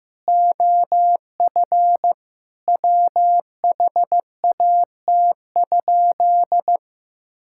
Morse code, heard as a keyed tone, about 15 wpm, 700 Hz. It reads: OF WHAT?
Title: Sentences from Top 1000 Words - Review Part 5 - 15wpm